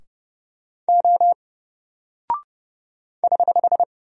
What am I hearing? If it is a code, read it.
Morse code, heard as a keyed tone, about 32 wpm, 700 Hz. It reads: O <HH>
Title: Single Character (Letter:Number:Punctuation:Prosign) Alt 30wpm